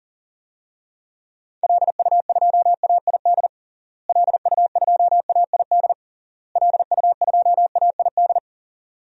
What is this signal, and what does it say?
Morse code, heard as a keyed tone, about 40 wpm, 700 Hz. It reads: LU2AID LU2AID LU2AID
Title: DXCC-100 Call Signs - 6 Characters - Repeated 3x - 40wpm